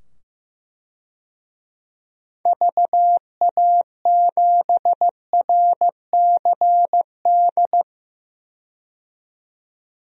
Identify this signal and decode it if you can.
Morse code, heard as a keyed tone, about 15 wpm, 700 Hz. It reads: VA7RCD
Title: DXCC-100 Call Signs - 6 Characters - 15wpm